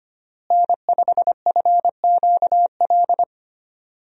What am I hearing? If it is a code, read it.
Morse code, heard as a keyed tone, about 25 wpm, 700 Hz. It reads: N5FQL
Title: USA Call Signs - 5 Characters - Repeated 2x - 25wpm